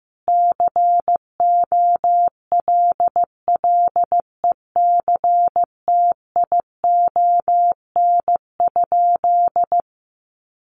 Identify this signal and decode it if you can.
Morse code, heard as a keyed tone, about 15 wpm, 700 Hz. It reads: COLLECTION?
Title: Sentences from Top 1200 Words 15wpm